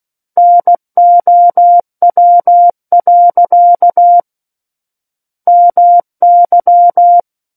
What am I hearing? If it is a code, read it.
Morse code, heard as a keyed tone, about 16 wpm, 700 Hz. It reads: NOW. MY
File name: Book One / 07-Chapter 7 - 16 WPM